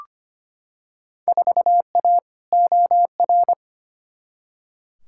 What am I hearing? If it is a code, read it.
Morse code, heard as a keyed tone, about 25 wpm, 700 Hz. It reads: 4A OR
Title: Field Day Exchanges Encore - 25wpm